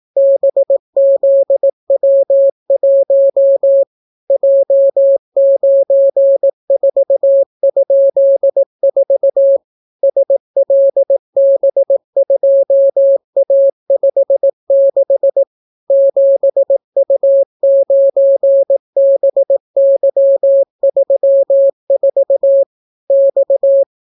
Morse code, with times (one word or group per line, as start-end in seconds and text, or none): 0.2-3.8 BZW1
4.3-9.6 J94?4
10.0-15.4 SLB2A56
15.9-22.6 7U9BY34
23.1-23.8 X